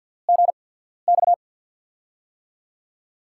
Morse code, heard as a keyed tone, about 47 wpm, spaced as for 13 wpm, 700 Hz. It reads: GX
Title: USA Call Signs - 6 Characters - 8x Character Spacing - 50wpm